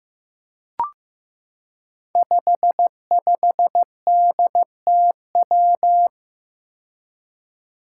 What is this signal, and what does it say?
Morse code, heard as a keyed tone, about 15 wpm, 700 Hz. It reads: 55DTW
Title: Five Challenging Letter-Number Combinations 15wpm